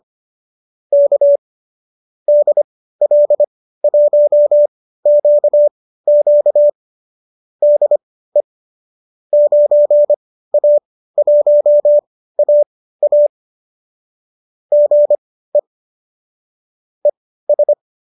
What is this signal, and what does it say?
Morse code, heard as a keyed tone, about 25 wpm, spaced as for 15 wpm, 590 Hz. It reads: K DL1QQ DE 9A1AA GE ES
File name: qso219 15